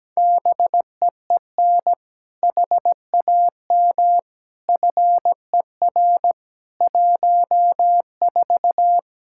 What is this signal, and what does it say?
Morse code, heard as a keyed tone, about 17 wpm, 700 Hz. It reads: BEEN HAM FER 14
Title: Practice QSO - Exchange 3 - More Info - 17wpm